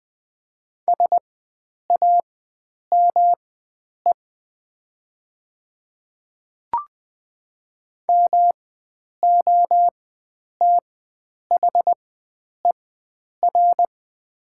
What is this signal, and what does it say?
Morse code, heard as a keyed tone, about 20 wpm, spaced as for 9 wpm, 700 Hz. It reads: SAME MOTHER
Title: 4x Character Spacing - Top 300 Words - 20wpm